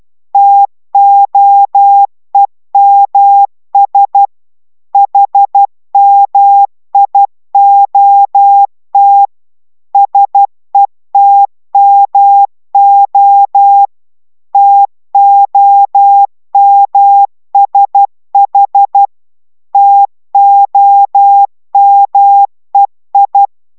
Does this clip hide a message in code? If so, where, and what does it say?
Yes: Morse code from 0.3 to 23.5 s, heard as a keyed tone, about 12 wpm, 800 Hz. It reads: TOEMS HMIOT SETMO TOMSH TOMEI